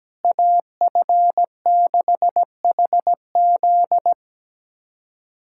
Morse code, heard as a keyed tone, about 17 wpm, 700 Hz. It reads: AF6HZ
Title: USA Callsign DE Callsign 17wpm